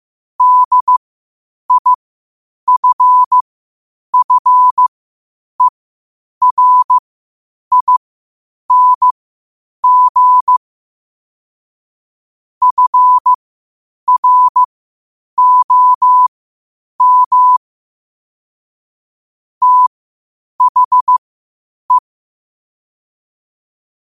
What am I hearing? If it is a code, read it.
Morse code, heard as a keyed tone, about 15 wpm, spaced as for 8 wpm, 1000 Hz. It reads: DIFFERING FROM THE
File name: qotdfiles / 8wpm